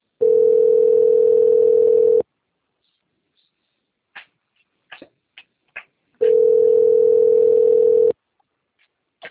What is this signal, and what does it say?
Ringback tone (North America), 440 + 480 Hz.